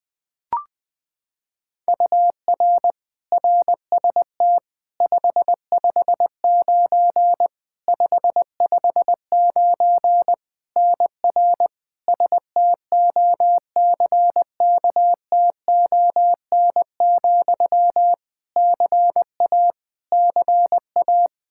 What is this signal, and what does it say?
Morse code, heard as a keyed tone, about 21 wpm, 700 Hz. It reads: UR RST 559 559 NR STOCKTON, CA CA
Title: Practice QSO - Exchange 1 - Beginning - 20wpm